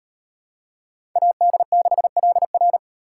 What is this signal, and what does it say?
Morse code, heard as a keyed tone, about 38 wpm, 700 Hz. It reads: AD6LR